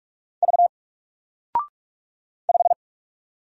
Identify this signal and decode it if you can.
Morse code, heard as a keyed tone, about 46 wpm, 700 Hz. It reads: V 5
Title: Single Letter-Number Alt 45wpm